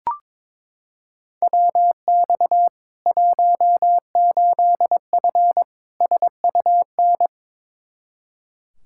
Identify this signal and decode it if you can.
Morse code, heard as a keyed tone, about 22 wpm, 700 Hz. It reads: WX 18F SUN
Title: Practice QSO - Exchange 2 - Info - 22wpm